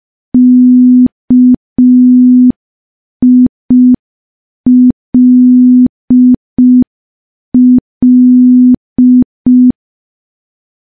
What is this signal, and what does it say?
Morse code, heard as a keyed tone, about 5 wpm, 250 Hz. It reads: KILL
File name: morse / morse-training-05WPM